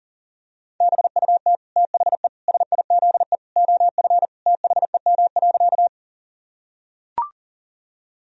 Morse code, heard as a keyed tone, about 37 wpm, 700 Hz. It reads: BUT THE SIZE OF THEM.
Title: Sentences from Top 1000 Words - Review Part 2 - 40wpm